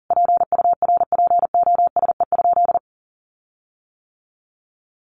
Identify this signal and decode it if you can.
Morse code, heard as a keyed tone, about 40 wpm, 700 Hz. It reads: PURPOSE?
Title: Sentences from Top 1500 Words 40wpm